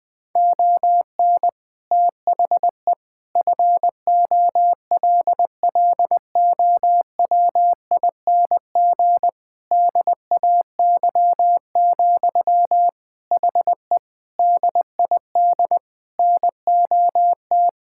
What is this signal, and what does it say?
Morse code, heard as a keyed tone, about 21 wpm, 700 Hz. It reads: ON THE FOLLOWING DAY, HE DID NOT
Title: Sentences from Top 1000 Words - Review Part 6 - 20wpm